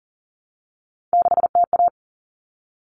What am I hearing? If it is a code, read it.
Morse code, heard as a keyed tone, about 40 wpm, 700 Hz. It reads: 6TA
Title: Common Cut Numbers - 3-Digit 40wpm